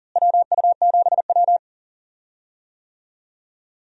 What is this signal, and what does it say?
Morse code, heard as a keyed tone, about 40 wpm, 700 Hz. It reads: WU7W
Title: USA Call Signs - 4 Characters - Repeated 3x - 40wpm